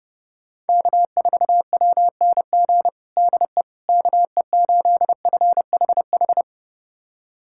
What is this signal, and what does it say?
Morse code, heard as a keyed tone, about 30 wpm, 700 Hz. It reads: K4WNG DE KE8FHH
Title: USA Callsign DE Callsign 30wpm